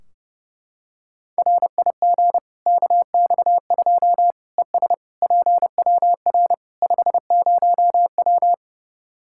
Morse code, heard as a keyed tone, about 32 wpm, 700 Hz. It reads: RIG KX2 ES PWR 50W
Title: Practice QSO - Exchange 2 - Info - 30wpm